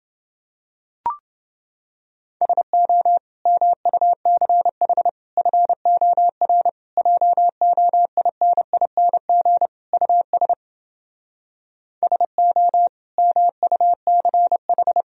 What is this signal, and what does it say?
Morse code, heard as a keyed tone, about 29 wpm, 700 Hz. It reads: SO MUCH FOR JOINING US SO MUCH